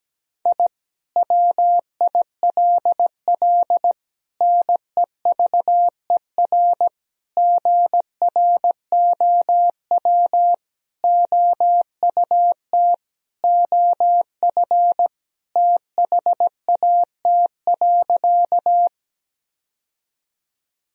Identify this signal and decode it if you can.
Morse code, heard as a keyed tone, about 17 wpm, 700 Hz. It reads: I WILL NEVER GROW OUT OF THAT.
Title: Sentences from Top 1000 Words - Review Part 1 - 17wpm